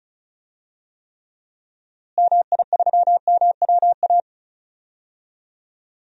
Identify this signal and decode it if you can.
Morse code, heard as a keyed tone, about 34 wpm, 700 Hz. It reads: MI3MWA
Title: DXCC-100 Call Signs - 6 Characters - Repeated 3x - 35wpm